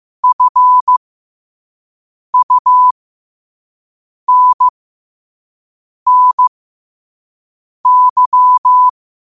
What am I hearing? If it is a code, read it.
Morse code, heard as a keyed tone, about 15 wpm, spaced as for 5 wpm, 1000 Hz. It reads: FUNNY
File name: qotdfiles / 5wpm